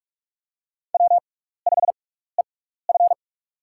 Morse code, heard as a keyed tone, about 42 wpm, spaced as for 15 wpm, 700 Hz. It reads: W5EF
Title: USA Call Signs - 4 Characters - 6x Character Spacing - 45wpm